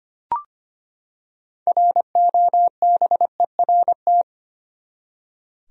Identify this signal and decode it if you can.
Morse code, heard as a keyed tone, about 25 wpm, 700 Hz. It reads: ROBERT
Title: USA Common Names - 25wpm